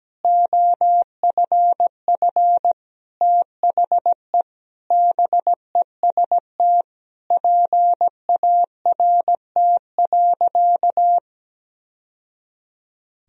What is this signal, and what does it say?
Morse code, heard as a keyed tone, about 17 wpm, 700 Hz. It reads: OFF THE BEST PART.
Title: Sentences from Top 1000 Words - Review Part 1 - 17wpm